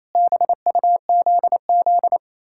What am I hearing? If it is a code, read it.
Morse code, heard as a keyed tone, about 35 wpm, 700 Hz. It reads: BUZZ
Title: Speed-Racing Common QSO Phrases - 28wpm